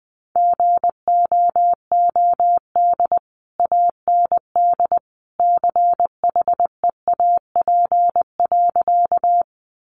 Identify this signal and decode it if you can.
Morse code, heard as a keyed tone, about 20 wpm, 700 Hz. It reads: GOOD AND CHEAP.
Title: Sentences from Top 1200 Words 20wpm